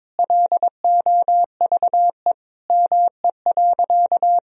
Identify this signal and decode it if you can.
Morse code, heard as a keyed tone, about 21 wpm, 700 Hz. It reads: LOVE ME.
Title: Sentences from Top 400 Words and Abbreviations 22wpm